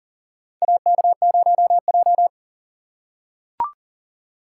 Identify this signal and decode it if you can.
Morse code, heard as a keyed tone, about 37 wpm, 700 Hz. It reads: AK0J
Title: USA Call Signs - 4 Characters - 6x Character Spacing - 40wpm